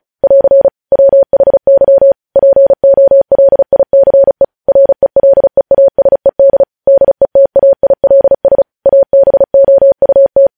Morse code, heard as a keyed tone, about 36 wpm, 550 Hz. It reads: <AR> WHY POLICE RELEASED DETAILS ABOUT
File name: podcast / News headlines-1x-35x35